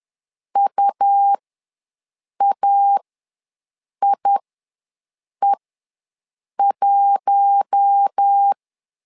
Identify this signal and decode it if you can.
Morse code, heard as a keyed tone, about 10 wpm, spaced as for 6 wpm, 800 Hz. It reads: UAIE1
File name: LEZIONI / 4 Q1 ws 10wpm 2538